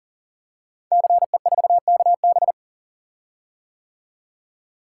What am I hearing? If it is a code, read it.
Morse code, heard as a keyed tone, about 42 wpm, 700 Hz. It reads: CE4KB
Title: Code Groups - 5 Letter:Numbers - 40wpm